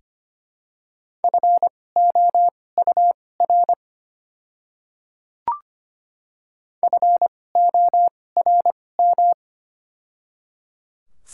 Morse code, heard as a keyed tone, about 25 wpm, spaced as for 18 wpm, 700 Hz. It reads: FOUR FORM
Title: 2x Character Spacing - Top 300 Words - 25wpm